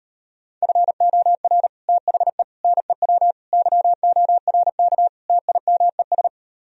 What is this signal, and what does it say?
Morse code, heard as a keyed tone, about 38 wpm, 700 Hz. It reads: FOR THE NEW YORK TIMES